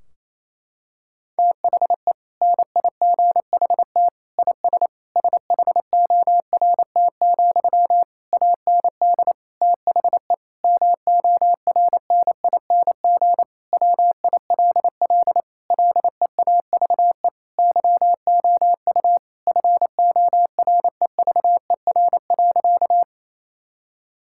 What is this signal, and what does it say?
Morse code, heard as a keyed tone, about 28 wpm, 700 Hz. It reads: THE NIGHT IS SHORT, AND THE MORNING WILL LEAVE YOU FOREVER.
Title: Sentences from Top 1600 Words 28wpm